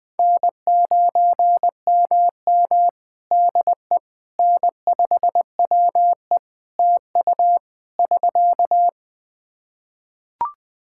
Morse code, heard as a keyed tone, about 21 wpm, 700 Hz. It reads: N9MM DE N5WE TU <SK>
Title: Practice QSO - Exchange 4 - Ending - 20wpm